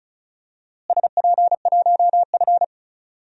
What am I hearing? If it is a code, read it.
Morse code, heard as a keyed tone, about 36 wpm, 700 Hz. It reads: SP1F
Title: DXCC-100 Call Signs - 4 Characters - Repeated 2x - 35wpm